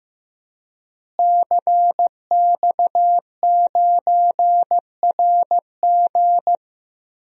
Morse code, heard as a keyed tone, about 15 wpm, 700 Hz. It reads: CX9RG
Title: DXCC-100 Call Signs - 5 Characters - Repeated 2x - 15wpm